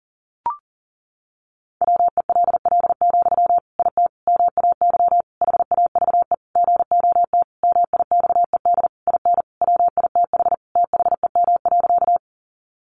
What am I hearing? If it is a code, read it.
Morse code, heard as a keyed tone, about 40 wpm, 700 Hz. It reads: WELL, IT MAY HAVE GOT MIXED IN WITH THEM.
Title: Sentences from Top 1500 Words 40wpm